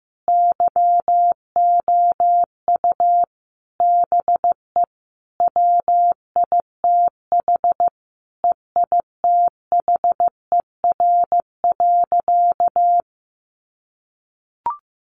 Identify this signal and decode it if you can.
Morse code, heard as a keyed tone, about 15 wpm, 700 Hz. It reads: YOU BE WITH EITHER.
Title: Sentences from Top 1000 Words 15wpm